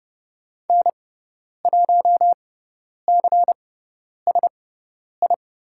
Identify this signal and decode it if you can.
Morse code, heard as a keyed tone, about 30 wpm, spaced as for 10 wpm, 700 Hz. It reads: N1CSI
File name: Lesson 18g - Letter m - Call Signs 30wpm